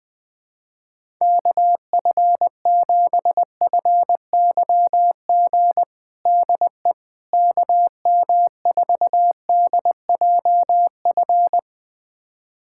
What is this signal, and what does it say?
Morse code, heard as a keyed tone, about 20 wpm, 700 Hz. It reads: KF7FYG DE KM4DJF
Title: USA Callsign DE Callsign 20wpm